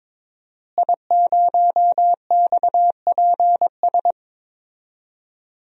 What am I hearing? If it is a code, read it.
Morse code, heard as a keyed tone, about 21 wpm, 700 Hz. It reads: I0XPS
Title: DXCC-100 Call Signs - 5 Characters - Repeated 2x - 22wpm